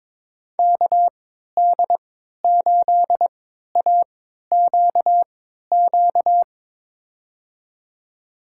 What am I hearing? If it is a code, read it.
Morse code, heard as a keyed tone, about 21 wpm, spaced as for 12 wpm, 700 Hz. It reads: KD8AQQ
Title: USA Call Signs - 6 Characters - 3x Character Spacing - 22wpm